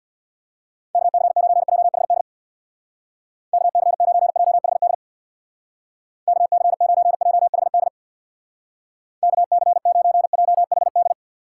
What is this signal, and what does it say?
Morse code, heard as a keyed tone, about 61 wpm, 700 Hz. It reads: KC9JHD KC9JHD KC9JHD KC9JHD
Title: Speed Racing - USA 2x3 Call Signs 50wpm